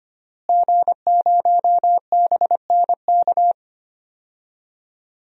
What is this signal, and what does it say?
Morse code, heard as a keyed tone, about 25 wpm, 700 Hz. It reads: G0BNK
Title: DXCC-100 Call Signs - 5 Characters - Repeated 2x - 25wpm